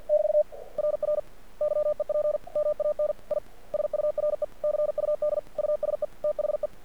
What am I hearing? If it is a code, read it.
Morse code, heard as a keyed tone, about 53 wpm, 600 Hz. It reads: <BT> RR YEP MAN I SURE CUD USE THE
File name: archived / CRSnet